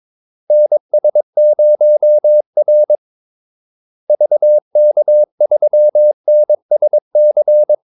Morse code, heard as a keyed tone, about 22 wpm, 600 Hz. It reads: NS0R VK3NSC
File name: Callsigns-22wpm